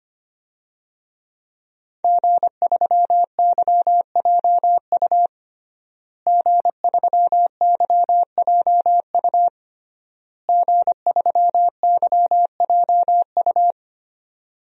Morse code, heard as a keyed tone, about 25 wpm, 700 Hz. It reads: G3YJU G3YJU G3YJU
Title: DXCC-100 Call Signs - 5 Characters - Repeated 3x - 25wpm